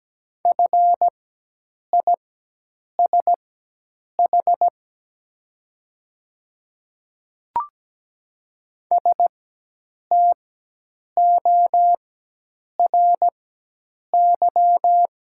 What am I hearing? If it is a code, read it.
Morse code, heard as a keyed tone, about 17 wpm, spaced as for 8 wpm, 700 Hz. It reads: FISH STORY
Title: 4x Character Spacing - Top 300 Words - 17wpm